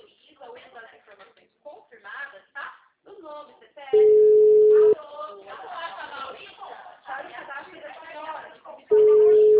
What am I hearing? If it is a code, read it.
Ringback tone (Europe, 425 Hz), with speech.
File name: out- -2030- - - .83363